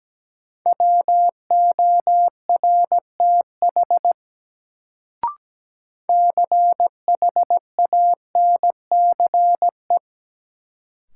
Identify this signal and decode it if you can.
Morse code, heard as a keyed tone, about 17 wpm, 700 Hz. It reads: WORTH CHANCE
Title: Sentences from Top 800 Words - New Word Drill - 17wpm